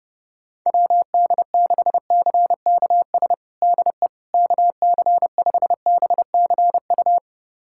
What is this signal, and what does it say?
Morse code, heard as a keyed tone, about 29 wpm, 700 Hz. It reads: WD6CKS DE KC5BCU
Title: USA Callsign DE Callsign - 6 Characters - 30wpm